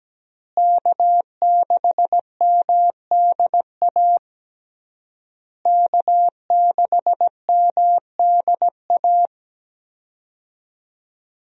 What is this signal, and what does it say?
Morse code, heard as a keyed tone, about 17 wpm, 700 Hz. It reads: K6MDA K6MDA
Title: USA Call Signs - 5 Characters - Repeated 3x - 17wpm